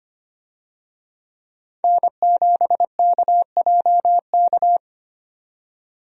Morse code, heard as a keyed tone, about 25 wpm, 690 Hz. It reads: N7KJK
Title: USA Call Signs - 5 Characters - 25wpm